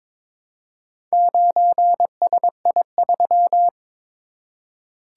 Morse code, heard as a keyed tone, about 22 wpm, 700 Hz. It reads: 9SI3
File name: practice / 22wpm / Four Letter-Number Combinations - 22wpm